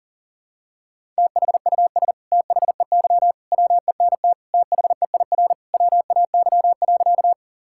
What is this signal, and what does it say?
Morse code, heard as a keyed tone, about 40 wpm, 700 Hz. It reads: THUS THEY WENT THEIR WAY.
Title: Sentences from Top 1000 Words - Review Part 5 - 40wpm